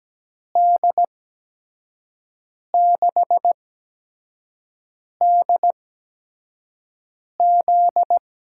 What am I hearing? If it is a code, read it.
Morse code, heard as a keyed tone, about 17 wpm, spaced as for 5 wpm, 700 Hz. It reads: D6DZ
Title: USA Call Signs - 6 Characters - 8x Character Spacing - 17wpm